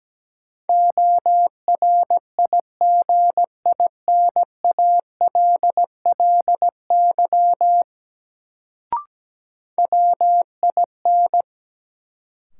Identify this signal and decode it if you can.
Morse code, heard as a keyed tone, about 17 wpm, 700 Hz. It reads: ORIGINALLY WIN
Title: Sentences from Top 1500 Words - New Word Drill - 17wpm